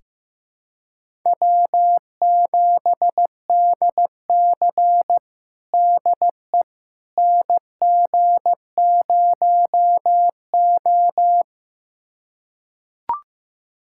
Morse code, heard as a keyed tone, about 15 wpm, 700 Hz. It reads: W7DC DE NG0O
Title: USA Callsign DE Callsign - 4 Characters - 15wpm